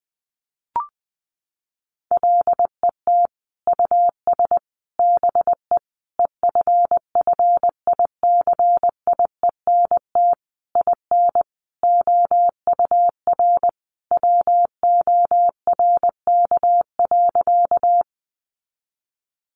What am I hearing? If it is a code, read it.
Morse code, heard as a keyed tone, about 20 wpm, 700 Hz. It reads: LET US BE EFFICIENT IN OUR WORK.
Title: Sentences from Top 1300 Words 20wpm